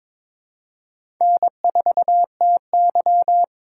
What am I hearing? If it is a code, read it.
Morse code, heard as a keyed tone, about 22 wpm, 700 Hz. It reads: N4TY
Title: USA Call Signs - 4 Characters - 3x Character Spacing - 22wpm